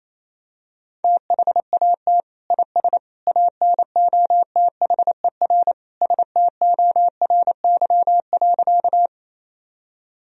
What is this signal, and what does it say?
Morse code, heard as a keyed tone, about 27 wpm, 700 Hz. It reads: THAT IS ANOTHER STORY.
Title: Sentences from Top 1000 Words - Review Part 1 - 28wpm